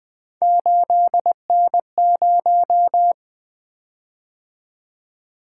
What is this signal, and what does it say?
Morse code, heard as a keyed tone, about 20 wpm, 700 Hz. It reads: 8N0
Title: Three Letter-Number Combinations - 20wpm